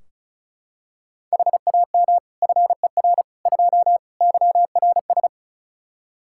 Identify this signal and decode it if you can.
Morse code, heard as a keyed tone, about 34 wpm, 700 Hz. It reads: HAM FER 2 YRS
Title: Practice QSO - Exchange 3 - More Info - 35wpm